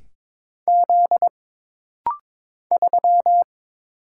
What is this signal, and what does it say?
Morse code, heard as a keyed tone, about 22 wpm, 700 Hz. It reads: Z 3
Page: This rapid-fire timing is geared to push your ICR (Instant Character Recognition) skills.